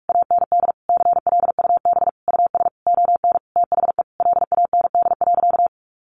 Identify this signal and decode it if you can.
Morse code, heard as a keyed tone, about 44 wpm, 700 Hz. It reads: AND CLUB US ON THE LAND.
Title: Sentences from Top 1200 Words 45wpm